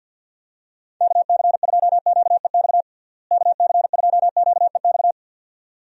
Morse code, heard as a keyed tone, about 51 wpm, 700 Hz. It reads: KC2QEX KC2QEX
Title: USA Call Signs - 6 Characters - Repeated 2x - 50wpm